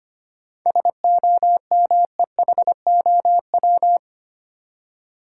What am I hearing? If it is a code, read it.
Morse code, heard as a keyed tone, about 25 wpm, 700 Hz. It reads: SOMEHOW
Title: Seven Letter Words - 25wpm